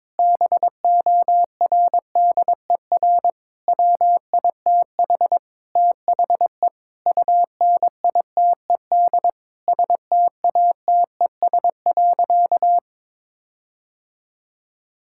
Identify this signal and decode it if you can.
Morse code, heard as a keyed tone, about 22 wpm, 700 Hz. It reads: BORDER WITH THE UNITED STATES.
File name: practice / 22wpm / Sentences with 4-Word Phrases - 22wpm